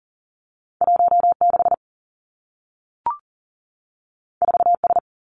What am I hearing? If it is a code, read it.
Morse code, heard as a keyed tone, about 42 wpm, 700 Hz. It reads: 1 6 4 S
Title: Two Challenging Letter-Number Combinations 40wpm